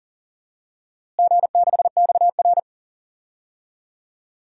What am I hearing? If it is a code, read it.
Morse code, heard as a keyed tone, about 40 wpm, 700 Hz. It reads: G6XR
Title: DXCC-100 Call Signs - 4 Characters - Repeated 3x - 40wpm